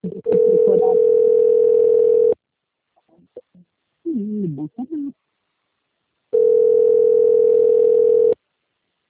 Ringback tone (North America), 440 + 480 Hz.